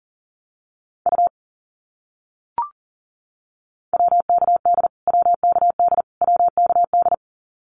Morse code, heard as a keyed tone, about 40 wpm, 700 Hz. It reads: U WKD WKD WKD
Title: Abbreviations Repeated 3x - 40wpm